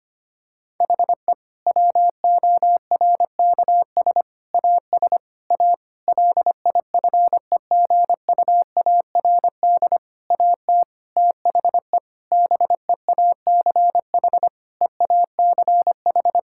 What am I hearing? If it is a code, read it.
Morse code, heard as a keyed tone, about 25 wpm, 700 Hz. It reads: HE WORKS AS A LIFEGUARD AT THE BEACH EACH
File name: practice / 25wpm / Difficult Sentences with Hints - 25wpm